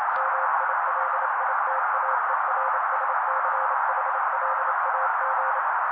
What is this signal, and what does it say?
Morse code, heard as a keyed tone, about 29 wpm, 530 Hz. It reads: MILITAERSCHLAG
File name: daily / 27wpm noise